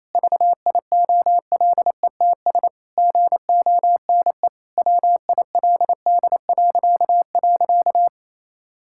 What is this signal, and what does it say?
Morse code, heard as a keyed tone, about 28 wpm, 700 Hz. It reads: VIOLETS GONE WILD..
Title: Pangram Challenge v2 - 28wpm